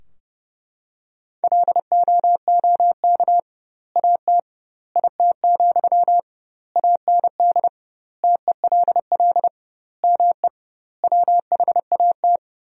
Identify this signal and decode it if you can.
Morse code, heard as a keyed tone, about 32 wpm, 700 Hz. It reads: LOOK AT IT, AND TELL ME WHAT
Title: 2x Word-Spacing - Sentences from Top 200 Words - 30wpm